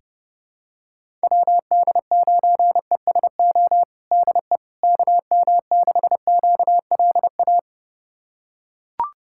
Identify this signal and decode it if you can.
Morse code, heard as a keyed tone, about 30 wpm, 700 Hz. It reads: WD9ESO DE KM6QLA
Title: USA Callsign DE Callsign 30wpm